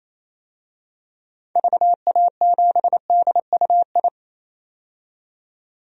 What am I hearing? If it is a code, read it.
Morse code, heard as a keyed tone, about 28 wpm, 700 Hz. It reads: VA7DUI
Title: DXCC-100 Call Signs - 6 Characters - 28wpm